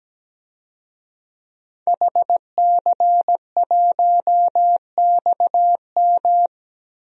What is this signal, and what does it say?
Morse code, heard as a keyed tone, about 17 wpm, 700 Hz. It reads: HC1XM
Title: DXCC-100 Call Signs - 5 Characters - Repeated 3x - 17wpm